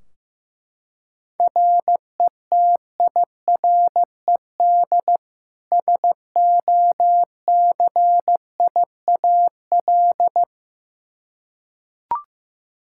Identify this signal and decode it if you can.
Morse code, heard as a keyed tone, about 15 wpm, 700 Hz. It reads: RETIRED SOCIAL
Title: Practice QSO - Exchange 3 - More Info - 15wpm